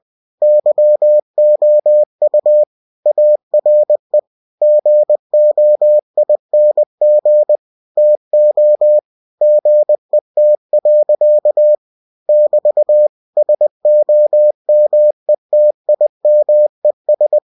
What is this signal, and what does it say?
Morse code, heard as a keyed tone, about 20 wpm, 600 Hz. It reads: YOU ARE GOING TO GET. <BT> SOMETIMES